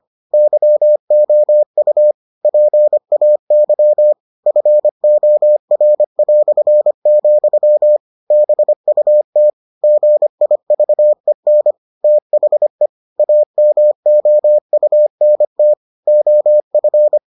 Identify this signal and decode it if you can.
Morse code, heard as a keyed tone, about 25 wpm, 600 Hz. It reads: YOU PAY FOR", BUT GIVEN THE AMOUNT OF